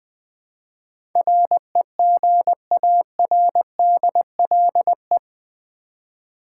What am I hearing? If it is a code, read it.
Morse code, heard as a keyed tone, about 20 wpm, 700 Hz. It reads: REGARDLE
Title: Build A Word - 4 Letters At A Time - 10-Letter Words - 20wpm